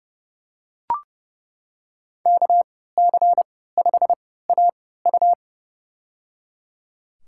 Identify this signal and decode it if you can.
Morse code, heard as a keyed tone, about 30 wpm, spaced as for 17 wpm, 700 Hz. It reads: KC5AU
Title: USA Call Signs - 3x Character Spacing - 30wpm